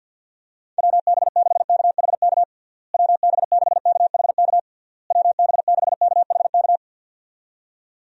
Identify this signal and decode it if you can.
Morse code, heard as a keyed tone, about 52 wpm, 700 Hz. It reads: WB6KHK WB6KHK WB6KHK
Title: USA Call Signs - 6 Characters - Repeated 3x - 50wpm